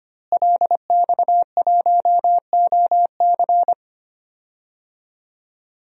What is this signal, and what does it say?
Morse code, heard as a keyed tone, about 25 wpm, 700 Hz. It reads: LX1OC
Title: DXCC-100 Call Signs - 5 Characters - Repeated 2x - 25wpm